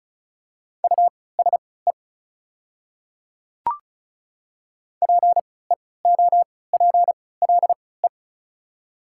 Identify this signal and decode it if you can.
Morse code, heard as a keyed tone, about 34 wpm, spaced as for 20 wpm, 690 Hz. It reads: USE PEOPLE
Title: 3x Character Spacing - Top 100 Words - 35wpm